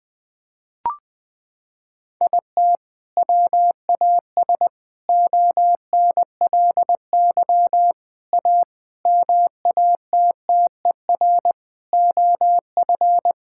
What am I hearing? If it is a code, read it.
Morse code, heard as a keyed tone, about 19 wpm, 700 Hz. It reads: IT WAS ONLY A MATTER OF
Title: Sets of 8 Words - 20wpm